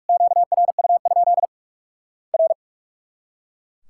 Morse code, heard as a keyed tone, about 44 wpm, 700 Hz. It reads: QRU? R
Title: Worked All Europe DX Contest - QTC Exchange - 45wpm